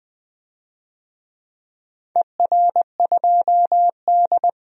Morse code, heard as a keyed tone, about 20 wpm, 700 Hz. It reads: ER2D
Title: DXCC-100 Call Signs - 4 Characters - 20wpm